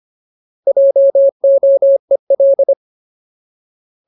Morse code, heard as a keyed tone, about 25 wpm, 550 Hz. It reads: JOEL
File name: Ultimate Word List / 25WPM Ultimate